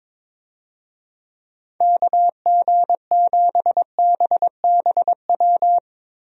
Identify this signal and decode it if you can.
Morse code, heard as a keyed tone, about 23 wpm, 700 Hz. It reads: KG7BBW
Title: USA Call Signs - 6 Characters - Repeated 3x - 22wpm